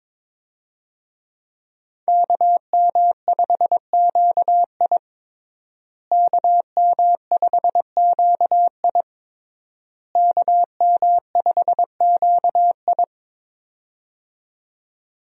Morse code, heard as a keyed tone, about 22 wpm, 700 Hz. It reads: KM5QI KM5QI KM5QI
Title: USA Call Signs - 5 Characters - Repeated 3x - 22wpm